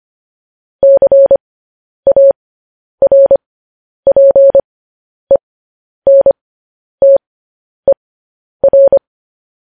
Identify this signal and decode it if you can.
Morse code, heard as a keyed tone, about 25 wpm, spaced as for 10 wpm, 550 Hz. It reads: CARPENTER
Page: news-headlines-in-morse-code-at-10-wpm